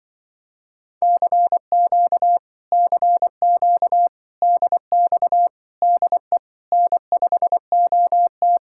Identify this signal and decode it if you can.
Morse code, heard as a keyed tone, about 24 wpm, 700 Hz. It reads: CQ CQ DX DE N5OT
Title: Speed-Racing Common QSO Phrases Encore - 20wpm